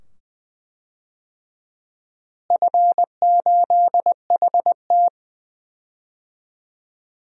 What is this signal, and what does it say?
Morse code, heard as a keyed tone, about 20 wpm, 700 Hz. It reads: F8HT
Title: DXCC-100 Call Signs - 4 Characters - Repeated 2x - 20wpm